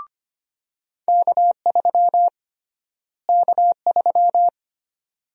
Morse code, heard as a keyed tone, about 25 wpm, 700 Hz. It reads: K3 K3
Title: USA Call Sign Prefixes - Repeated 2x - 25wpm